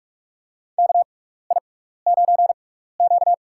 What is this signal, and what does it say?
Morse code, heard as a keyed tone, about 46 wpm, spaced as for 16 wpm, 700 Hz. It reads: KI9Q
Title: USA Call Signs - 4 Characters - 6x Character Spacing - 45wpm